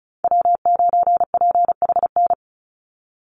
Morse code, heard as a keyed tone, about 34 wpm, 700 Hz. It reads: W9PHN
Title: USA Call Signs 35wpm